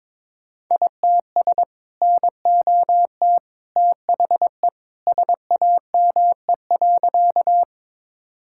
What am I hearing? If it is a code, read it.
Morse code, heard as a keyed tone, about 22 wpm, 700 Hz. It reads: ITS NOT THE SAME.
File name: practice / 22wpm / Sentences with 2-Word Phrases - 22wpm